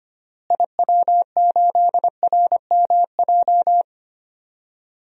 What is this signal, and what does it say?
Morse code, heard as a keyed tone, about 25 wpm, 700 Hz. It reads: IW8RMJ
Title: DXCC-100 Call Signs - 6 Characters - Repeated 2x - 25wpm